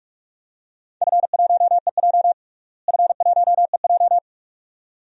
Morse code, heard as a keyed tone, about 47 wpm, 700 Hz. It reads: F1EJ F1EJ
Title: DXCC-100 Call Signs - 4 Characters - Repeated 2x - 45wpm